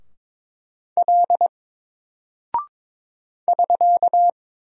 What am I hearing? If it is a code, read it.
Morse code, heard as a keyed tone, about 22 wpm, 700 Hz. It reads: L <SK>
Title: Single Character (Letter:Number:Punctuation:Prosign) 22wpm